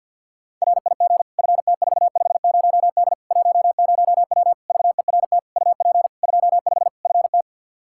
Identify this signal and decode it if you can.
Morse code, heard as a keyed tone, about 47 wpm, 700 Hz. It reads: RIG FT450D 10W VERT UP 25 FT
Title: Practice QSO - Exchange 2 - Info - 50wpm